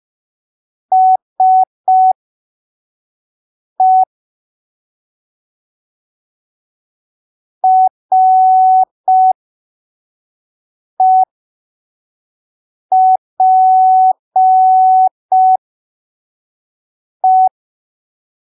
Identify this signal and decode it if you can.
Morse code, heard as a keyed tone, about 5 wpm, spaced as for 3 wpm, 750 Hz. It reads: SE REPE
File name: arquivos / texto / texto 08 05ppm